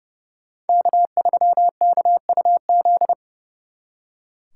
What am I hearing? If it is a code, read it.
Morse code, heard as a keyed tone, about 39 wpm, 700 Hz. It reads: K3KUZ